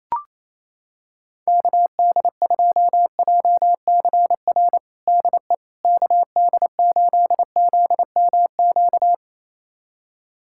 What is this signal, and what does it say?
Morse code, heard as a keyed tone, about 28 wpm, 700 Hz. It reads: KD2JCR DE KD8ZMQ
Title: USA Callsign DE Callsign 28wpm